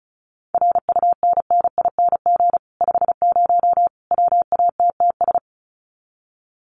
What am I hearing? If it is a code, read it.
Morse code, heard as a keyed tone, about 34 wpm, 700 Hz. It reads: RUNNING 50 WATTS
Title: Common QSO Phrases 35wpm